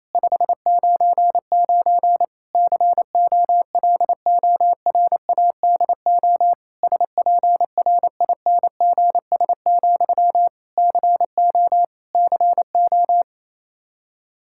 Morse code, heard as a keyed tone, about 28 wpm, 700 Hz. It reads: 599 COLORADO SPRINGS, CO CO
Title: K1USN SST QSO Practice 28wpm